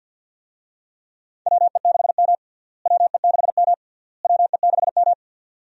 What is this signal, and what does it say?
Morse code, heard as a keyed tone, about 50 wpm, 700 Hz. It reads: WE6M WE6M WE6M
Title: USA Call Signs - 4 Characters - Repeated 3x - 50wpm